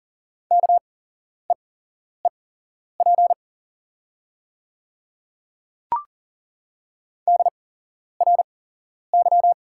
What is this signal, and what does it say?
Morse code, heard as a keyed tone, about 41 wpm, spaced as for 11 wpm, 700 Hz. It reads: KEEP DRY
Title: 8x Character Spacing - Top 500 Words - 40wpm